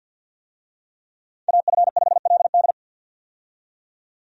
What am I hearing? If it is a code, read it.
Morse code, heard as a keyed tone, about 52 wpm, 700 Hz. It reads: AF5LD
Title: USA Call Signs - 5 Characters - 50wpm